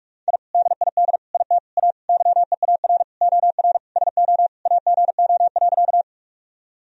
Morse code, heard as a keyed tone, about 44 wpm, 700 Hz. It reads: I DID IT A YEAR OR SO AGO.
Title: Sentences from Top 1000 Words - Review Part 3 - 45wpm